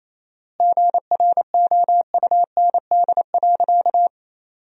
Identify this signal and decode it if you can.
Morse code, heard as a keyed tone, about 28 wpm, 700 Hz. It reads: GROUND.
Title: 4x Word-Spacing - Sentences from Top 500 Words - 28wpm